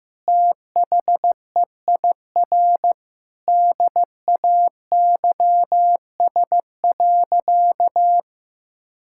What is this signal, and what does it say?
Morse code, heard as a keyed tone, about 15 wpm, 700 Hz. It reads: THEIR DAYS.
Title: Sentences from Top 1000 Words - Review Part 5 - 15wpm